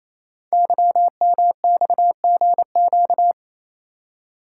Morse code, heard as a keyed tone, about 27 wpm, 700 Hz. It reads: YMXGQ
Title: Code Groups - 5 Letters - 28wpm